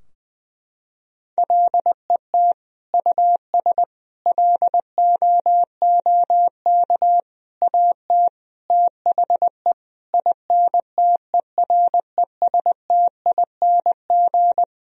Morse code, heard as a keyed tone, about 20 wpm, 700 Hz. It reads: LET US LOOK AT THE INTERESTING
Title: Sentences from Top 1000 Words - Review Part 2 - 20wpm